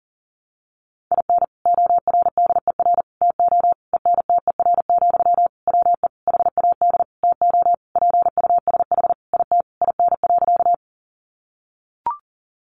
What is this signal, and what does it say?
Morse code, heard as a keyed tone, about 40 wpm, 700 Hz. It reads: IN ORDER TO ENTER, WE HAD TO PUSH IT IN.
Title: Sentences from Top 1300 Words 40wpm